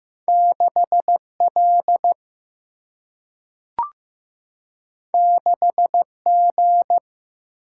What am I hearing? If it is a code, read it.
Morse code, heard as a keyed tone, about 15 wpm, 700 Hz. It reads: 6L 6G
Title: Single-Letter Single-Number 15wpm